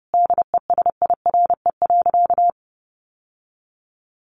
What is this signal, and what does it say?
Morse code, heard as a keyed tone, about 30 wpm, 700 Hz. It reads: DESIRE.
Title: Sentences from Top 1100 Words 30wpm